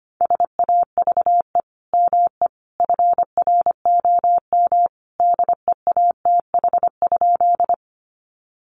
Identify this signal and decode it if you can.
Morse code, heard as a keyed tone, about 25 wpm, 700 Hz. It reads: SAVE ME FROM DEATH?
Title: Sentences from Top 1100 Words 25wpm